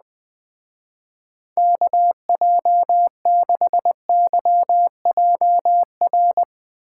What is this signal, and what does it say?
Morse code, heard as a keyed tone, about 20 wpm, 700 Hz. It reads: KJ6YJR
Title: Speed Racing - USA 2x3 Call Signs 17wpm